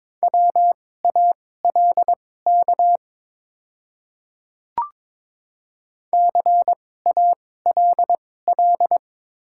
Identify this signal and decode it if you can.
Morse code, heard as a keyed tone, about 22 wpm, spaced as for 16 wpm, 700 Hz. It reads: WALK CALL
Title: 2x Character Spacing - Top 300 Words - 22wpm